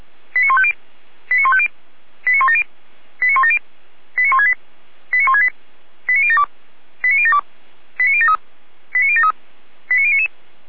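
Five-tone selective call (ZVEI), heard as quick runs of five tones, tones of about 70 ms. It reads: 89150 89150 89170 89170 89168 89168 89072 89072 89073 89073 89099
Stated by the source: ZVEI five-tone selective call 89169 received by SDR.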